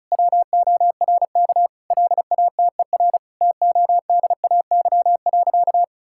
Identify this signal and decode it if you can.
Morse code, heard as a keyed tone, about 36 wpm, 700 Hz. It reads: WORK LATER TODAY.
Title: Sentences from Top 1000 Words - Review Part 6 - 35wpm